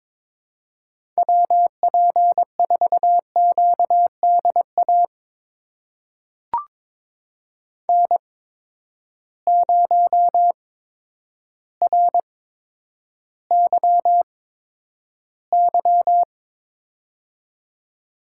Morse code, heard as a keyed tone, about 22 wpm, spaced as for 10 wpm, 700 Hz. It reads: WP4QDA N0RYY
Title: USA Call Signs - 8x Character Spacing - 22wpm